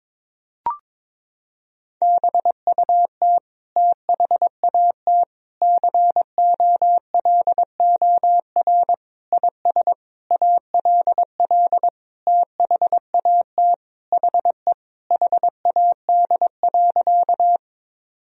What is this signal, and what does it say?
Morse code, heard as a keyed tone, about 21 wpm, 700 Hz. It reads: BUT THAT COLOR IS ALL THAT HE HAD.
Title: Sentences from Top 1000 Words - Review Part 2 - 22wpm